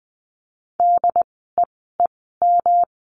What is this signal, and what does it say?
Morse code, heard as a keyed tone, about 20 wpm, spaced as for 14 wpm, 700 Hz. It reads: DEEM
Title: 108 Lesson 18f - Letter m - Words 20wpm